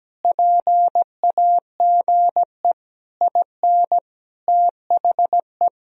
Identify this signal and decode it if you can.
Morse code, heard as a keyed tone, about 17 wpm, 700 Hz. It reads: PAGE IN THE
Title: Sentences from Top 1000 Words - Review Part 1 - 17wpm